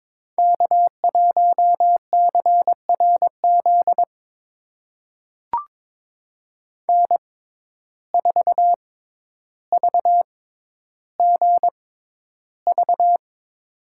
Morse code, heard as a keyed tone, about 21 wpm, spaced as for 11 wpm, 700 Hz. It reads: K1CRZ N4VGV
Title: USA Call Signs - 5 Characters - 6x Character Spacing - 22wpm